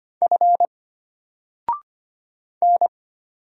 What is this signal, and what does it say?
Morse code, heard as a keyed tone, about 25 wpm, 700 Hz. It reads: F N
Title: Single Character (Letter:Number:Punctuation:Prosign) Alt 25wpm